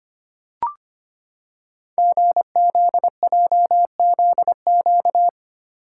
Morse code, heard as a keyed tone, about 25 wpm, 700 Hz. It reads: GZJZQ
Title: Code Groups - 5 Letter:Numbers - 25wpm